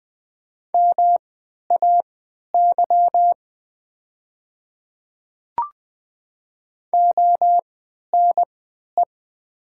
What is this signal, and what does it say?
Morse code, heard as a keyed tone, about 20 wpm, spaced as for 11 wpm, 700 Hz. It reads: MAY ONE
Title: 3x Character Spacing - Top 100 Words - 20wpm